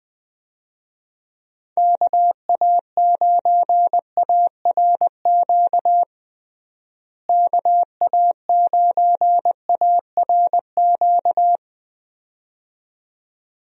Morse code, heard as a keyed tone, about 20 wpm, 700 Hz. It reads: KA9ARQ KA9ARQ
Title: USA Call Signs - 6 Characters - Repeated 2x - 20wpm